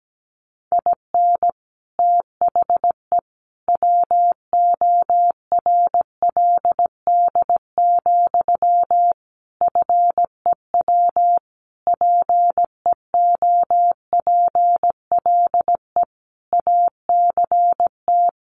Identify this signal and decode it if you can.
Morse code, heard as a keyed tone, about 17 wpm, 700 Hz. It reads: IN THE WORLD, FEW PEOPLE ACT
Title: Sentences from Top 1400 Words 17wpm